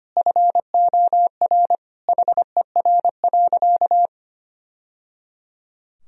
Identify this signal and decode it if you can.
Morse code, heard as a keyed tone, about 25 wpm, 700 Hz. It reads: FOR HER.
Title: Sentences from Top 1000 Words - Review Part 3 - 25wpm